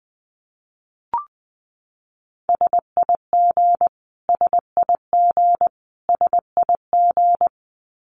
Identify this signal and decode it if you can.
Morse code, heard as a keyed tone, about 20 wpm, 700 Hz. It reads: SIG SIG SIG
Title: Abbreviations Repeated 3x - 20wpm